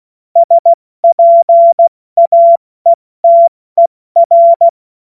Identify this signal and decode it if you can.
Morse code, heard as a keyed tone, about 16 wpm, spaced as for 14 wpm, 680 Hz. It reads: SPAETER
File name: Morsekurs / Witze / Witz 05 März 2024 70 F80